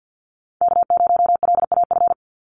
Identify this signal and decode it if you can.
Morse code, heard as a keyed tone, about 50 wpm, 700 Hz. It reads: K0LAF
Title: NA SOTA Chaser Callsigns 50wpm